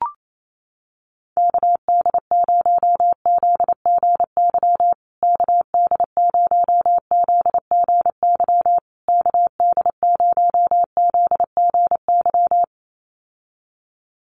Morse code, heard as a keyed tone, about 28 wpm, 700 Hz. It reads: KD0ZGY KD0ZGY KD0ZGY
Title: USA Call Signs Repeated 3x 28wpm